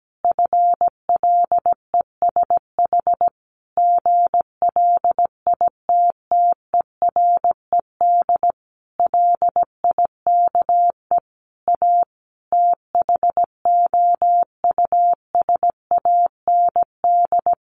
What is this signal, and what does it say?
Morse code, heard as a keyed tone, about 17 wpm, 700 Hz. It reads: FLESH GLITTERED LIKE A THOUSAND
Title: A Sound Of Thunder 17wpm